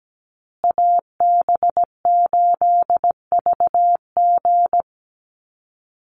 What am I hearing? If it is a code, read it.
Morse code, heard as a keyed tone, about 17 wpm, 700 Hz. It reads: AB8VG
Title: USA Call Signs Repeated 3x 17wpm